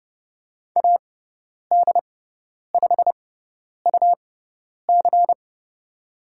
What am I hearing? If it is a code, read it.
Morse code, heard as a keyed tone, about 30 wpm, spaced as for 10 wpm, 700 Hz. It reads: AD5UC
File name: Lesson 17g - Letter u - Call Signs 30wpm